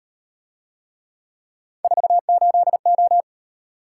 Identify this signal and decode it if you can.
Morse code, heard as a keyed tone, about 36 wpm, 700 Hz. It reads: 48O